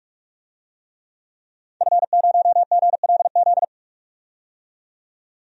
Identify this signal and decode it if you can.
Morse code, heard as a keyed tone, about 41 wpm, 700 Hz. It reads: F0GLZ
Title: DXCC-100 Call Signs - 5 Characters - 45wpm